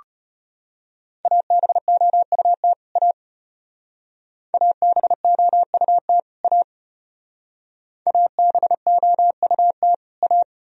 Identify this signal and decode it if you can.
Morse code, heard as a keyed tone, about 32 wpm, 700 Hz. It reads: ABOUT A ABOUT A ABOUT A